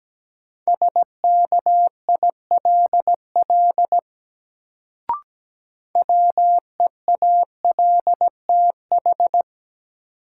Morse code, heard as a keyed tone, about 17 wpm, 700 Hz. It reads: SKILL WEALTH
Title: Sentences from Top 1200 Words - New Word Drill Alt - 17wpm